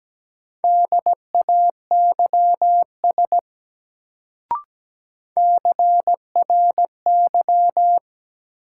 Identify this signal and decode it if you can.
Morse code, heard as a keyed tone, about 17 wpm, 700 Hz. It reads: DAYS CRY
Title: Sets of 5 Words Encore - New Word Drill - 17wpm